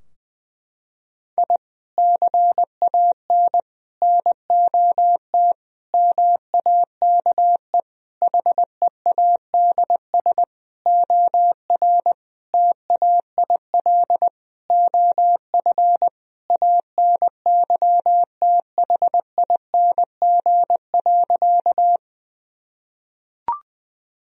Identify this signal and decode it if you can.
Morse code, heard as a keyed tone, about 20 wpm, 700 Hz. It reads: I CAN NOT MAKE HEADS OR TAIL OF ANYTHING.
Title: Sentences from Top 1000 Words - Review Part 3 - 20wpm